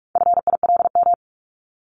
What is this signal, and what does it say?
Morse code, heard as a keyed tone, about 44 wpm, 700 Hz. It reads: FILM
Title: Top 501-750 Words 45wpm